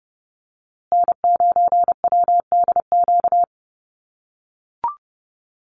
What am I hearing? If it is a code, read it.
Morse code, heard as a keyed tone, about 30 wpm, 700 Hz. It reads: N9WDQ
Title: USA Call Signs Repeated 2x 30wpm